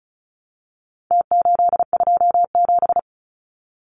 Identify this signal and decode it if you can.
Morse code, heard as a keyed tone, about 34 wpm, 700 Hz. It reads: T827
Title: Numbers - Contest Style - 4-Digit 35wpm